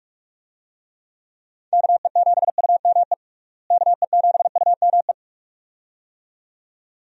Morse code, heard as a keyed tone, about 46 wpm, 700 Hz. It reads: KE7UME KE7UME
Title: USA Call Signs - 6 Characters - Repeated 2x - 45wpm